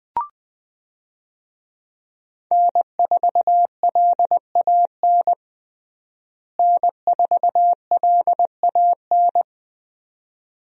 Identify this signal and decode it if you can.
Morse code, heard as a keyed tone, about 20 wpm, 700 Hz. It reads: N4LAN N4LAN
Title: USA Call Signs - 5 Characters - Repeated 2x - 20wpm